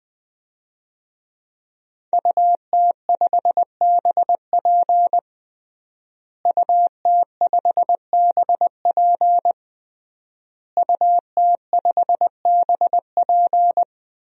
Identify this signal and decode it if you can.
Morse code, heard as a keyed tone, about 21 wpm, 700 Hz. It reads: UT5BP UT5BP UT5BP
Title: DXCC-100 Call Signs - 5 Characters - Repeated 3x - 20wpm